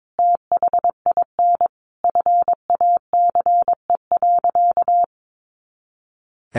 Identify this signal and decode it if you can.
Morse code, heard as a keyed tone, about 22 wpm, 700 Hz. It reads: THIN FACE.
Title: A Sound Of Thunder 22wpm